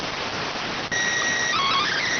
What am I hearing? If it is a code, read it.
SSTV (slow-scan television), PD 290.